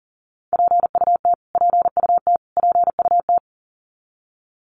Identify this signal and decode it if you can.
Morse code, heard as a keyed tone, about 40 wpm, 700 Hz. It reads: PUT PUT PUT
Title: Top 500 Words Repeated 40wpm